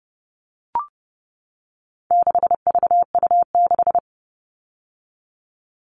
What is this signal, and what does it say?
Morse code, heard as a keyed tone, about 30 wpm, 700 Hz. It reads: 6VU6
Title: All Cut Numbers - 4-Digit 30wpm